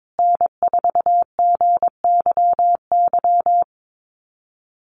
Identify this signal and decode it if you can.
Morse code, heard as a keyed tone, about 22 wpm, 700 Hz. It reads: N4GYY
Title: USA Call Signs Repeated 2x 22wpm